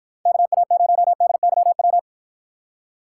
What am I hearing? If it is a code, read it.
Morse code, heard as a keyed tone, about 53 wpm, 700 Hz. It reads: KA0DYW